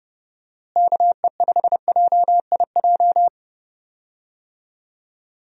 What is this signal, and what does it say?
Morse code, heard as a keyed tone, about 29 wpm, 700 Hz. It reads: KE5JIJ
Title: USA Call Signs - 6 Characters - Repeated 3x - 30wpm